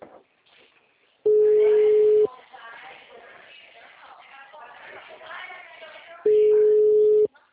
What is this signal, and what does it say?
Ringback tone (Europe, 425 Hz), with speech.
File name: out- -2009- - - .447